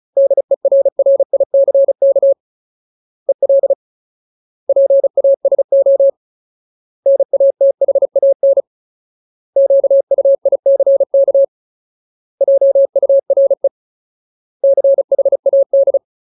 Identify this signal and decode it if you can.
Morse code, heard as a keyed tone, about 37 wpm, 550 Hz. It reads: DERRICK EL PASO NATHAN QUICK JURE CHAD
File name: Ultimate Word List / 35WPM Ultimate